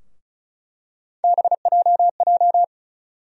Morse code, heard as a keyed tone, about 36 wpm, 700 Hz. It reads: BJJ
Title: USA Call Sign Postfixes - 3 Letters 35wpm